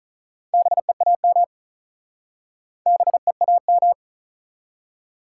Morse code, heard as a keyed tone, about 43 wpm, 700 Hz. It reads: BEAM BEAM
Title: Speed-Racing Top 180 QSO Elements 35wpm